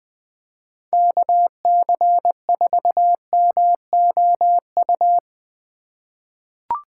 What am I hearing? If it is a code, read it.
Morse code, heard as a keyed tone, about 20 wpm, 700 Hz. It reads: KC4MOU
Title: USA Call Signs - 6 Characters - 4x Character Spacing - 20wpm